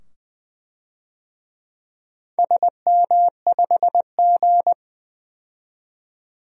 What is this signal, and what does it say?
Morse code, heard as a keyed tone, about 20 wpm, 700 Hz. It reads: SM5G
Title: DXCC-100 Call Signs - 4 Characters - 20wpm